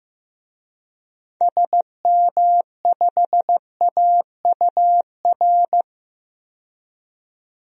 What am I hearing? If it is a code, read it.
Morse code, heard as a keyed tone, about 15 wpm, 700 Hz. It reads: SM5AUR
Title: DXCC-100 Call Signs - 6 Characters - 15wpm